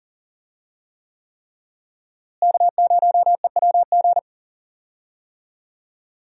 Morse code, heard as a keyed tone, about 40 wpm, 700 Hz. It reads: K0EWG
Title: USA Call Signs - 5 Characters - Repeated 3x - 40wpm